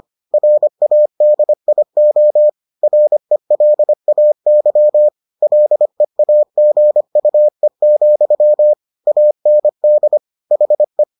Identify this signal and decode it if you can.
Morse code, heard as a keyed tone, about 25 wpm, 600 Hz. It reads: RADIO RELAY LEAGUE, AND HE